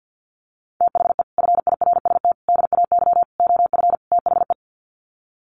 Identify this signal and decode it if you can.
Morse code, heard as a keyed tone, about 50 wpm, 700 Hz. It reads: THE FIRST DAY OF THE
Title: Sets of 5 Words Encore2 50wpm